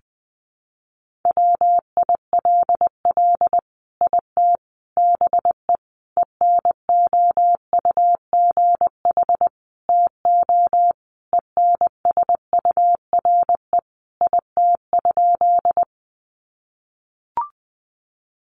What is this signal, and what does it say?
Morse code, heard as a keyed tone, about 20 wpm, 700 Hz. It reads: WILL IT BE ENOUGH TO ENSURE IT?
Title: Sentences from Top 1100 Words 20wpm